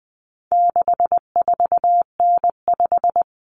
Morse code, heard as a keyed tone, about 20 wpm, 700 Hz. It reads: 64N5
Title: Common Cut Numbers - 4-Digit 20wpm